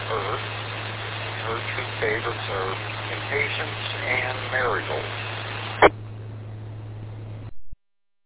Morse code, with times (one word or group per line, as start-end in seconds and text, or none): none